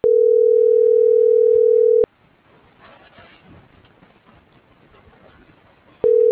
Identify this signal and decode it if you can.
Ringback tone (North America), 440 + 480 Hz.